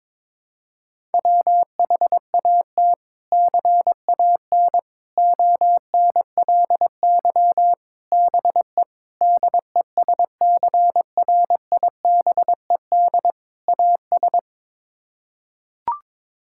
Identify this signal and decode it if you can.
Morse code, heard as a keyed tone, about 21 wpm, 700 Hz. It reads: WHAT CAN ONLY BE DESCRIBED AS
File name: practice / 22wpm / Sets of 6 Words - 22wpm